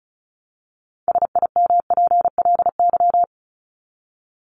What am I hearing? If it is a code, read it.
Morse code, heard as a keyed tone, about 34 wpm, 700 Hz. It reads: SIMPLY
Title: LY Suffix - Top 100 Words with Base Words 35wpm